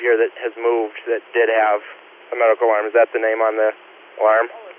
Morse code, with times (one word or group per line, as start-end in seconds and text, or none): none